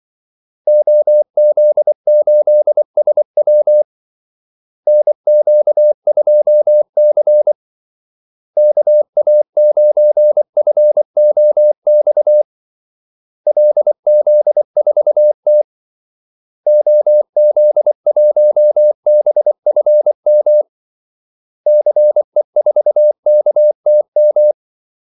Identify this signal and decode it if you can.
Morse code, heard as a keyed tone, about 24 wpm, 600 Hz. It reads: OZ8SW NQ2C KA9FOX LZ4T OZ1BFM CE4KTM
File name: Callsigns-24wpm